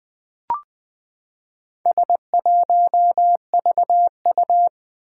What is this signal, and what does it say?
Morse code, heard as a keyed tone, about 20 wpm, 700 Hz. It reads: S1VU
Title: Four Challenging Letter-Number Combinations 20wpm